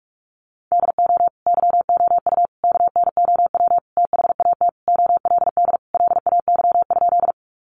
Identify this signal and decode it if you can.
Morse code, heard as a keyed tone, about 44 wpm, 700 Hz. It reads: DO YOU KNOW THAT OLD LAY?
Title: Sentences from Top 400 Words 45wpm